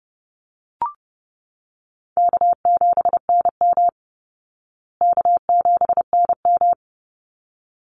Morse code, heard as a keyed tone, about 30 wpm, 700 Hz. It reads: K7NM K7NM
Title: Call Signs Repeated 2x - 30wpm